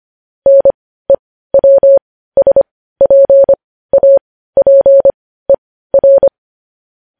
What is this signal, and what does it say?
Morse code, heard as a keyed tone, about 25 wpm, spaced as for 15 wpm, 550 Hz. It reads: NEWSPAPER
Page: News Headlines in Morse Code at 15 WPM